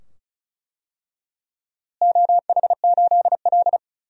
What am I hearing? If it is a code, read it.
Morse code, heard as a keyed tone, about 36 wpm, 700 Hz. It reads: OH8L
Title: DXCC-100 Call Signs - 4 Characters - Repeated 2x - 35wpm